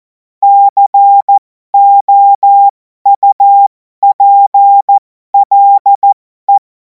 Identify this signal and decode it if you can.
Morse code, heard as a keyed tone, about 14 wpm, spaced as for 12 wpm, 800 Hz. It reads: COUPLE
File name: Morsekurs / Witze